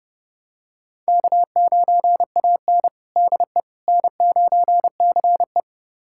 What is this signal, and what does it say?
Morse code, heard as a keyed tone, about 30 wpm, 700 Hz. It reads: K9AN DE N9CE
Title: USA Callsign DE Callsign - 4 Characters - 30wpm